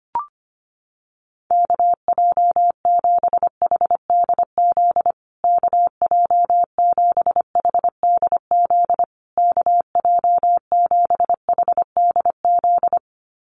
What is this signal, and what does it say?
Morse code, heard as a keyed tone, about 25 wpm, 700 Hz. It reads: KJ7HDZ KJ7HDZ KJ7HDZ
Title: USA Call Signs Repeated 3x 25wpm